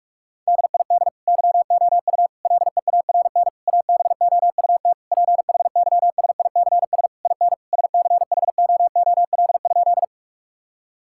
Morse code, heard as a keyed tone, about 46 wpm, 700 Hz. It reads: DID YOU LEARN ABOUT PHYSICS IN SCHOOL?
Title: Sentences from Top 1000 Words - Review Part 5 - 45wpm